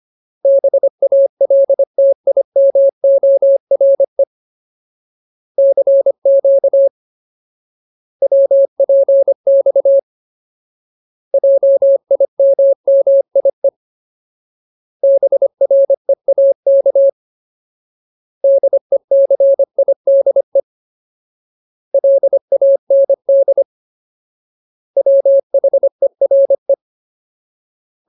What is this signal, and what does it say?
Morse code, heard as a keyed tone, about 25 wpm, 550 Hz. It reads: BALTIMORE CQ WPX JIMMIE BREAK DECIDE LAND WHERE
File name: Ultimate Word List / 25WPM Ultimate